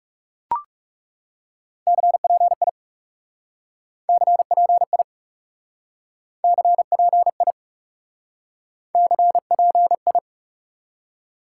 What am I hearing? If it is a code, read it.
Morse code, heard as a keyed tone, about 38 wpm, 700 Hz. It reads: CPI CPI CPI CPI